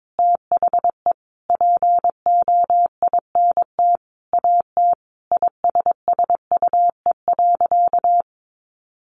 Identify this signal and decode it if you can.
Morse code, heard as a keyed tone, about 22 wpm, 700 Hz. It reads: THE POINT AT ISSUE.
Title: Sentences from Top 900 Words 22wpm